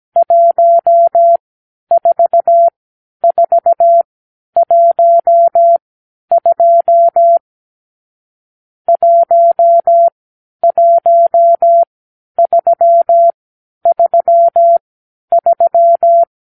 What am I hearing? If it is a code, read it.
Morse code, heard as a keyed tone, about 17 wpm, spaced as for 10 wpm, 680 Hz. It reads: 14412 11333
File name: k7qo / Lesson 46 (1-4)